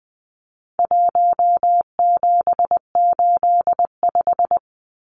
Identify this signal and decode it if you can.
Morse code, heard as a keyed tone, about 20 wpm, 700 Hz. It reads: 1785
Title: Numbers - 4-Digit 20wpm